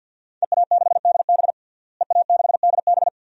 Morse code, heard as a keyed tone, about 51 wpm, 700 Hz. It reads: EA6DB EA6DB
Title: DXCC-100 Call Signs - 5 Characters - Repeated 3x - 50wpm